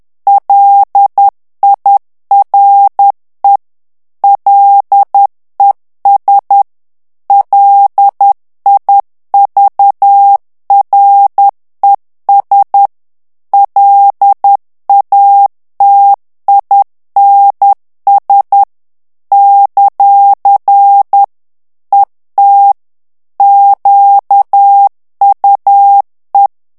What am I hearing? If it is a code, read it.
Morse code, heard as a keyed tone, about 10 wpm, 800 Hz. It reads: LIRE LES LIVRES LATINS ; ET QUE
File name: morse mp3 / FRANCAIS TEXTE CLAIR 2 VITESSE 10 MOTS MINUTES